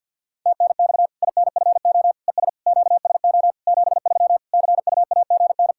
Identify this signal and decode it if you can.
Morse code, heard as a keyed tone, about 51 wpm, 700 Hz. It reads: TNX INFO ES QSO 73 CUAGN
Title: Practice QSO - Exchange 4 - Ending - 50wpm